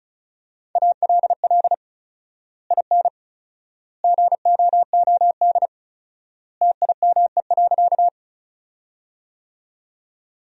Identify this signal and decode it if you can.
Morse code, heard as a keyed tone, about 34 wpm, 700 Hz. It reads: ALL IN GOOD TIME.
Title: 4x Word-Spacing - Sentences from Top 200 Words - 35wpm